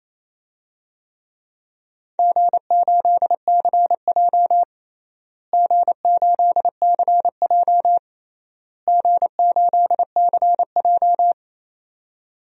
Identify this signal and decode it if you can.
Morse code, heard as a keyed tone, about 28 wpm, 700 Hz. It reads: G8CJ G8CJ G8CJ
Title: DXCC-100 Call Signs - 4 Characters - Repeated 3x - 28wpm